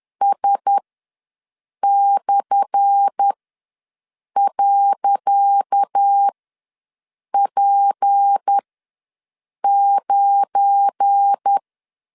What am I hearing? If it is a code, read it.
Morse code, heard as a keyed tone, about 10 wpm, spaced as for 6 wpm, 800 Hz. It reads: S/.P9
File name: LEZIONI / 10 SIGNS / 10C DOT ws 10wpm 4213